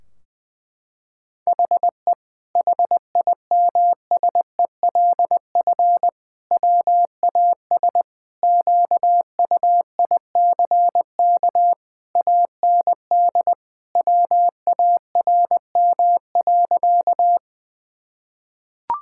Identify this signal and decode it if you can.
Morse code, heard as a keyed tone, about 20 wpm, 700 Hz. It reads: HE HIMSELF WAS QUICK AND WARM.
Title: Sentences from Top 1600 Words 20wpm